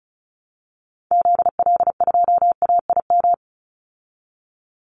Morse code, heard as a keyed tone, about 34 wpm, 700 Hz. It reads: ZL2AIM
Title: Call Signs Repeated 3x - 35wpm